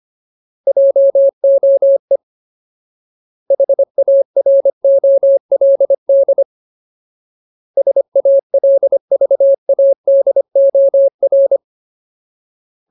Morse code, heard as a keyed tone, about 25 wpm, 550 Hz. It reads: JOE HAROLD SALVADOR
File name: Ultimate Word List / 25WPM Ultimate